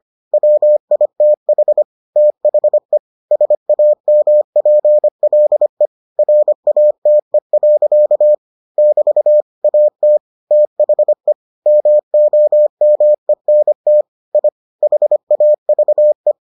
Morse code, heard as a keyed tone, about 25 wpm, 600 Hz. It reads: WITH THE SAMPLE RATE. <BT> AT THE MOMENT I HAVE